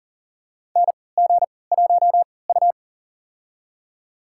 Morse code, heard as a keyed tone, about 40 wpm, spaced as for 23 wpm, 700 Hz. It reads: NG1U
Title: USA Call Signs - 4 Characters - 3x Character Spacing - 40wpm